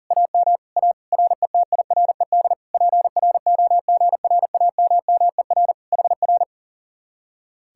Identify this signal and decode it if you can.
Morse code, heard as a keyed tone, about 52 wpm, 700 Hz. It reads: AM A RETIRED PROGRAMMER HR